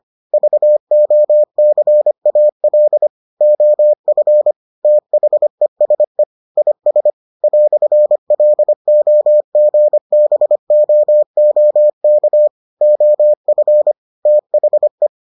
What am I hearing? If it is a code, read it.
Morse code, heard as a keyed tone, about 25 wpm, 600 Hz. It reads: VOCAL OF THESE IS "LOGBOOK OF THE